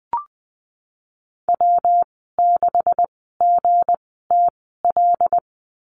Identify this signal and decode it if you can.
Morse code, heard as a keyed tone, about 20 wpm, spaced as for 15 wpm, 700 Hz. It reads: W6GTL
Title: 173 Lesson 28g - Number 9 - Call Signs 20wpm